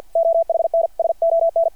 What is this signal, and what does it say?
Morse code, heard as a keyed tone, about 52 wpm, 660 Hz. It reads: OHNSON
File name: archived / CRSnet